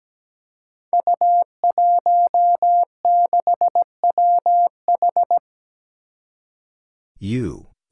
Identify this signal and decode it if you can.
Morse code, heard as a keyed tone, about 17 wpm, 700 Hz. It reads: U16WH
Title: Five Challenging Letter-Number Combinations 17wpm